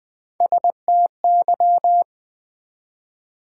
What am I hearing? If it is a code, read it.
Morse code, heard as a keyed tone, about 20 wpm, 700 Hz. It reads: STY
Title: Build A Word - 1 Letter At A Time - 3 to 5-Letter Words - 20wpm